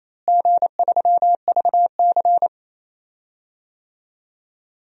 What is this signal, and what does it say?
Morse code, heard as a keyed tone, about 28 wpm, 700 Hz. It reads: G3VC
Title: DXCC-100 Call Signs - 4 Characters - Repeated 2x - 28wpm